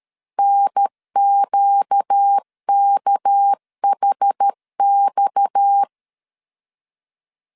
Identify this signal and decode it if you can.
Morse code, heard as a keyed tone, about 13 wpm, 800 Hz. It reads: NQKHX
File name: CW / DET / 1 6 1500C S A 12wpm 3440